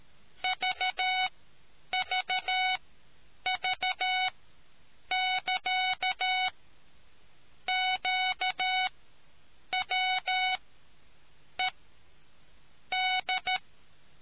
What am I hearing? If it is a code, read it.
Morse code, heard as a keyed tone, about 13 wpm, spaced as for 7 wpm, 740 Hz. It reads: VVV<KA>QWED